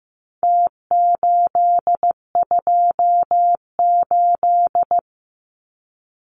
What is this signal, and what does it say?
Morse code, heard as a keyed tone, about 15 wpm, 700 Hz. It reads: T828
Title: Numbers - Contest Style - 4-Digit 15wpm